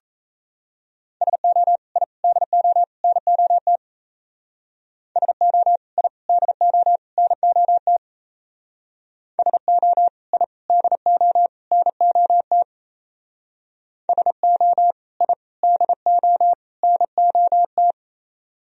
Morse code, heard as a keyed tone, about 35 wpm, 700 Hz. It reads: SO I DO NOT SO I DO NOT SO I DO NOT SO I DO NOT